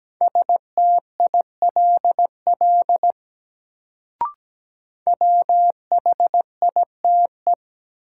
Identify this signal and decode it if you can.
Morse code, heard as a keyed tone, about 17 wpm, 700 Hz. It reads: STILL WHITE
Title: Sentences from Top 300 Words - New Word Drill - 17wpm